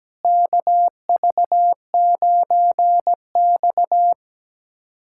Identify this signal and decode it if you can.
Morse code, heard as a keyed tone, about 17 wpm, 700 Hz. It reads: KV9X
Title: USA Callsign DE Callsign - 4 Characters - 17wpm